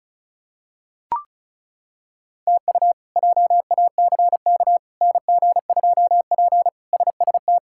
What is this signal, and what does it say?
Morse code, heard as a keyed tone, about 34 wpm, 700 Hz. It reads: TU JACK NG2P SST
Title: K1USN SST QSO Practice - 35wpm